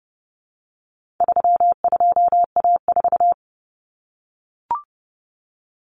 Morse code, heard as a keyed tone, about 30 wpm, 700 Hz. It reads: 32A4
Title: Common Cut Numbers - 4-Digit 30wpm